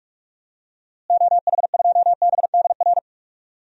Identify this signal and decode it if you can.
Morse code, heard as a keyed tone, about 42 wpm, 700 Hz. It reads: OH2BDR
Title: DXCC-100 Call Signs - 6 Characters - Repeated 2x - 45wpm